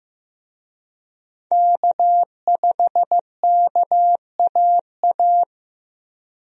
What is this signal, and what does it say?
Morse code, heard as a keyed tone, about 15 wpm, 700 Hz. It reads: K5KAA
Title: USA Call Signs - 5 Characters - Repeated 2x - 15wpm